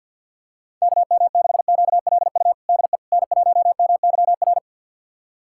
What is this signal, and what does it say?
Morse code, heard as a keyed tone, about 52 wpm, 700 Hz. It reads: KM6QLU DE N1MYR
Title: USA Callsign DE Callsign 50wpm